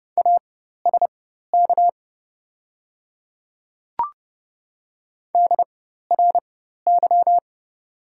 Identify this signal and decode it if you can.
Morse code, heard as a keyed tone, about 30 wpm, spaced as for 14 wpm, 700 Hz. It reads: ASK DRY
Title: 4x Character Spacing - Top 500 Words - 30wpm